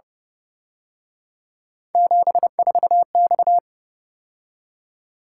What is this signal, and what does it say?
Morse code, heard as a keyed tone, about 29 wpm, 700 Hz. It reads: 74X
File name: Speed-Racing - Three Letter-Number Combinations - 25wpm